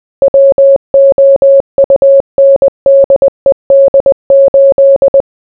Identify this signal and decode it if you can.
Morse code, heard as a keyed tone, about 20 wpm, 550 Hz. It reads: WOUNDED 8